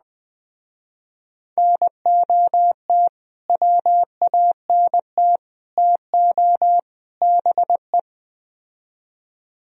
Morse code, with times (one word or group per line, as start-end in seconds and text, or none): 1.6-3.1 NOT
3.5-5.4 WANT
5.8-6.8 TO
7.2-8.0 BE